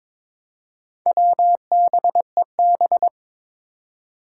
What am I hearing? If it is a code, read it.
Morse code, heard as a keyed tone, about 22 wpm, 700 Hz. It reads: WBEB
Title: Four Challenging Letter-Number Combinations 22wpm